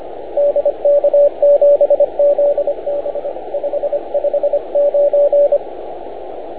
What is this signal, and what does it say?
Morse code, heard as a keyed tone, about 25 wpm, 580 Hz, with radio noise. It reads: DK7ZB 559